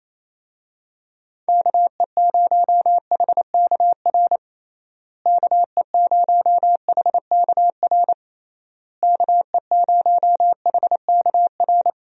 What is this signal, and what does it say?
Morse code, heard as a keyed tone, about 28 wpm, 700 Hz. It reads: KE0HKR KE0HKR KE0HKR
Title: USA Call Signs - 6 Characters - Repeated 3x - 28wpm